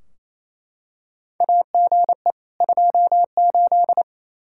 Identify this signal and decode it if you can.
Morse code, heard as a keyed tone, about 28 wpm, 700 Hz. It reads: AGE 28
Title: Practice QSO - Exchange 3 - More Info - 28wpm